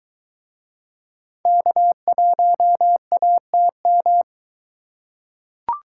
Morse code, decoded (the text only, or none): K1ATM